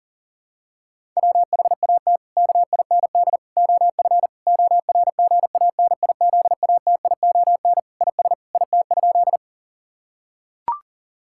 Morse code, heard as a keyed tone, about 40 wpm, 700 Hz. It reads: WHAT KIND OF ORGANIZATION IS IT?
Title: Sentences from Top 1000 Words - Review Part 5 - 40wpm